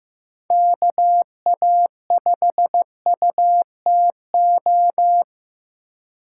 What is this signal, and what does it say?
Morse code, heard as a keyed tone, about 15 wpm, 700 Hz. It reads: KA5UTO
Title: USA Call Signs - 6 Characters - 3x Character Spacing - 15wpm